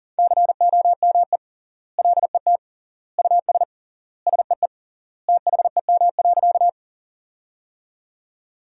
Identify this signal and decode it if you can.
Morse code, heard as a keyed tone, about 37 wpm, 700 Hz. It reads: COME LET US SEE THEM.
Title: 3x Word-Spacing - Sentences from Top 300 Words - 40wpm